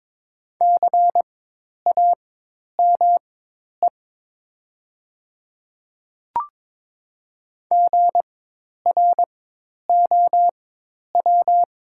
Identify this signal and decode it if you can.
Morse code, heard as a keyed tone, about 22 wpm, spaced as for 10 wpm, 700 Hz. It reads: CAME GROW
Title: 4x Character Spacing - Top 300 Words - 22wpm